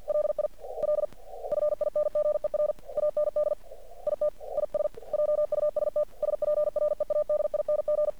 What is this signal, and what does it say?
Morse code, heard as a keyed tone, about 46 wpm, 600 Hz. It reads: LIL FINGER AND IT IS JUST SPREADING
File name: archived / CRSnet